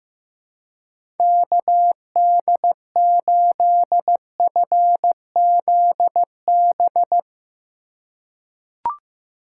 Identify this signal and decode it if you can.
Morse code, heard as a keyed tone, about 15 wpm, 700 Hz. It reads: KD8FZB
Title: USA Call Signs - 6 Characters - 4x Character Spacing - 15wpm